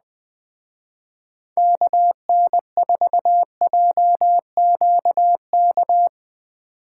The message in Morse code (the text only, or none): KN4JQK